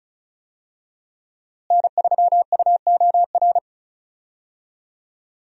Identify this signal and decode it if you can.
Morse code, heard as a keyed tone, about 36 wpm, 700 Hz. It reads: N3UOR
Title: USA Call Signs - 5 Characters - Repeated 2x - 35wpm